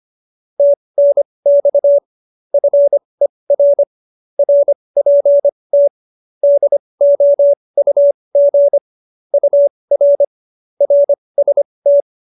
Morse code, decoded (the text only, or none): TNX FER RPT DOUG UR RST